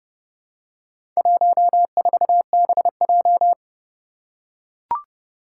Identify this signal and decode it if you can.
Morse code, heard as a keyed tone, about 30 wpm, 700 Hz. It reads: 14BJ
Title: Four Challenging Letter-Number Combinations 30wpm